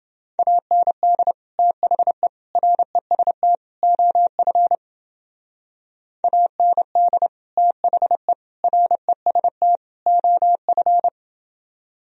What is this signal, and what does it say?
Morse code, heard as a keyed tone, about 29 wpm, 700 Hz. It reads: AND THE REST OF AND THE REST OF